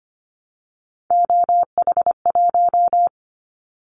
Morse code, heard as a keyed tone, about 25 wpm, 700 Hz. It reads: OH1
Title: Callsign Prefixes for Top 25 DXCC - 25wpm